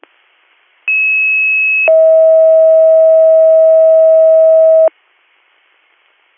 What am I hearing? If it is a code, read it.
Two-tone page, heard as a one-second tone, then a three-second tone, about 2600 then 640 Hz. It is heard from 0.9 to 4.9 s.